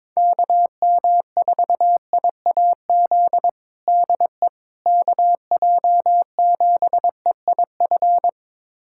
Morse code, heard as a keyed tone, about 22 wpm, 700 Hz. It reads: KM4IAZ DE KJ7EIF
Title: USA Callsign DE Callsign 22wpm